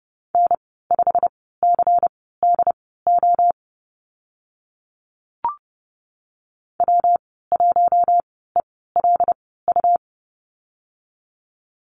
Morse code, heard as a keyed tone, about 30 wpm, spaced as for 17 wpm, 700 Hz. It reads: N5CDO W1ELU
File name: Lesson 19g - Letter w - Call Signs 30wpm